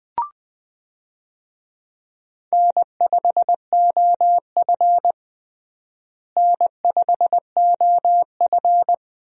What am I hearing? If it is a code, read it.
Morse code, heard as a keyed tone, about 20 wpm, 700 Hz. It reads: N5OF N5OF
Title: USA Call Signs - 4 Characters - Repeated 3x - 20wpm